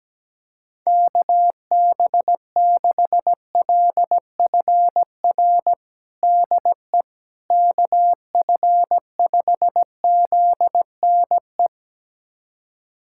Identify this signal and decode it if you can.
Morse code, heard as a keyed tone, about 17 wpm, 700 Hz. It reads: KB6LFR DE KF5ZNE
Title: USA Callsign DE Callsign 17wpm